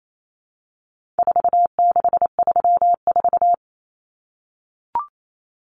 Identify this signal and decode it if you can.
Morse code, heard as a keyed tone, about 28 wpm, 700 Hz. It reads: 4634
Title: Numbers - 4-Digit 28wpm